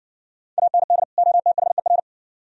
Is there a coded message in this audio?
Yes: Morse code from 0.6 to 2.0 s, heard as a keyed tone, about 57 wpm, 700 Hz. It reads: AND OTHER